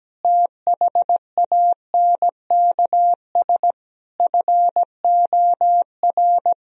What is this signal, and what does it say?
Morse code, heard as a keyed tone, about 17 wpm, 700 Hz. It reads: THANKS FOR
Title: Sentences from Top 1000 Words - Review Part 4 - 17wpm